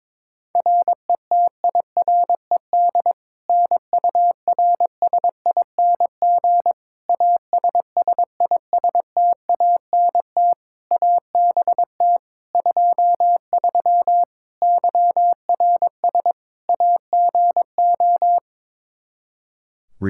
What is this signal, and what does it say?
Morse code, heard as a keyed tone, about 22 wpm, 700 Hz. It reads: RETIRED NURSING ASSISTANT ABT 23 YRS AGO
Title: Practice QSO - Exchange 3 - More Info - 22wpm